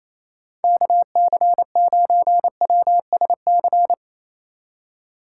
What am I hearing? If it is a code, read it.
Morse code, heard as a keyed tone, about 35 wpm, 700 Hz. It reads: KC9WSC